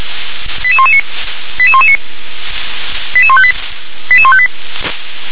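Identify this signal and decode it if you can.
Five-tone selective call (ZVEI), heard as quick runs of five tones, tones of about 70 ms. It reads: 88119 88119 88158 88158
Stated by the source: ZVEI five-tone selective call received on an SDR.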